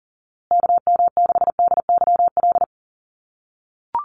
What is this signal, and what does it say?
Morse code, heard as a keyed tone, about 40 wpm, 700 Hz. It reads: KM6DYL
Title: USA Call Signs Repeated 3x 40wpm